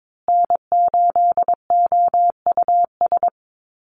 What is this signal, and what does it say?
Morse code, heard as a keyed tone, about 22 wpm, 700 Hz. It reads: N8OUS
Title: USA Call Signs Repeated 3x 22wpm